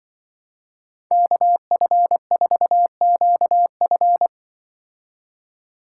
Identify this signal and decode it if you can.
Morse code, heard as a keyed tone, about 24 wpm, 700 Hz. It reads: KF4QF
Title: Speed Racing - USA 2x2 Call Signs 20wpm